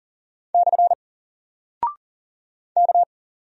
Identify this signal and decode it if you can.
Morse code, heard as a keyed tone, about 39 wpm, 700 Hz. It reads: / K
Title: Single Character (Letter:Number:Punctuation:Prosign) 40wpm